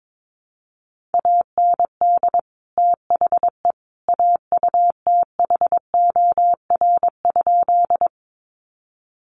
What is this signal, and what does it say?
Morse code, heard as a keyed tone, about 22 wpm, 700 Hz. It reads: AND THE AUTHOR?
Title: Sentences from Top 1200 Words 22wpm